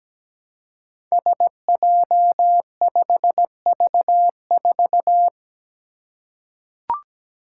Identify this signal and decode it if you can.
Morse code, heard as a keyed tone, about 17 wpm, 700 Hz. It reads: SJ5V4
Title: Five Challenging Letter-Number Combinations 17wpm